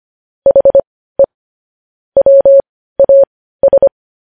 Morse code, heard as a keyed tone, about 25 wpm, spaced as for 15 wpm, 550 Hz. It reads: HE WAS
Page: news headlines in morse code at 15 wpm